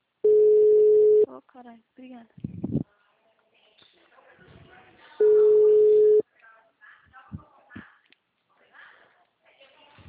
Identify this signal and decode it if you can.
Ringback tone (Europe, 425 Hz), with speech.